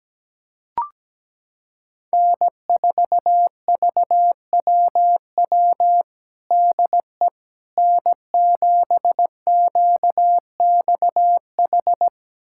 Morse code, heard as a keyed tone, about 17 wpm, 700 Hz. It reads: N4VWW DE N7QXH
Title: USA Callsign DE Callsign - 5 Characters - 17wpm